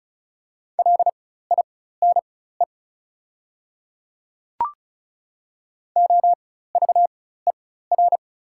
Morse code, heard as a keyed tone, about 35 wpm, spaced as for 16 wpm, 690 Hz. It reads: LINE OVER
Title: 4x Character Spacing - Top 200 Words - 35wpm